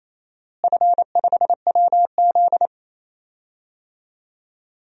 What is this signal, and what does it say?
Morse code, heard as a keyed tone, about 28 wpm, 700 Hz. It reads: F5WZ
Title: DXCC-100 Call Signs - 4 Characters - Repeated 3x - 28wpm